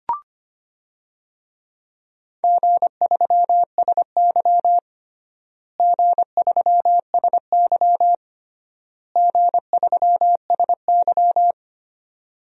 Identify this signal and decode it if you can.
Morse code, heard as a keyed tone, about 25 wpm, 700 Hz. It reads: G3SY G3SY G3SY
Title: DXCC-100 Call Signs - 4 Characters - Repeated 3x - 25wpm